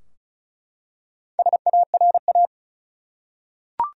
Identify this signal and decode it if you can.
Morse code, heard as a keyed tone, about 36 wpm, 700 Hz. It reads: SARA
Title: USA Common Names - Max 5 Letters - 35wpm